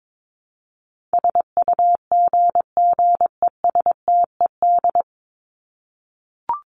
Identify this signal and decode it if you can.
Morse code, heard as a keyed tone, about 22 wpm, 700 Hz. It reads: SUGGESTED
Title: ED Suffix - Top 100 Words with Base Words 22wpm